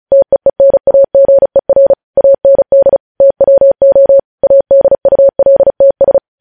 Morse code, heard as a keyed tone, about 36 wpm, 550 Hz. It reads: TEENAGER AND TWO ADULTS
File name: podcast / News headlines-1x-35x35